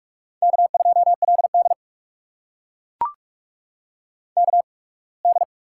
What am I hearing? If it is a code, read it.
Morse code, heard as a keyed tone, about 44 wpm, 700 Hz. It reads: K 2 L D K D
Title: USA Call Signs - 4 Characters - 8x Character Spacing - 45wpm